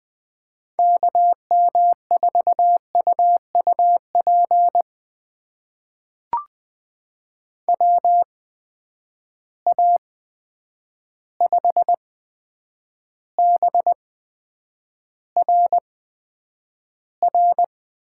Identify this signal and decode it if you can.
Morse code, heard as a keyed tone, about 20 wpm, spaced as for 9 wpm, 700 Hz. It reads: KM4UUP WA5BRR
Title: USA Call Signs - 6 Characters - 8x Character Spacing - 20wpm